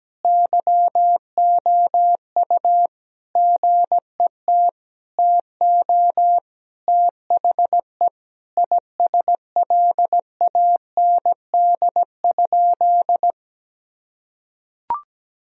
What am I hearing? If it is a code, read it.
Morse code, heard as a keyed tone, about 17 wpm, 700 Hz. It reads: YOU GET TO THE ISLAND?
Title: Sentences from Top 1000 Words - Review Part 3 - 17wpm